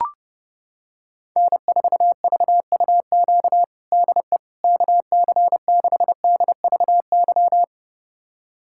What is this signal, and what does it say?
Morse code, heard as a keyed tone, about 28 wpm, 700 Hz. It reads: N4VUQ DE KC6DVY
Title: USA Callsign DE Callsign 30wpm